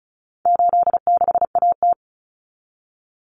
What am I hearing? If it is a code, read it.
Morse code, heard as a keyed tone, about 34 wpm, 700 Hz. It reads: 86AT
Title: Common Cut Numbers - 4-Digit 35wpm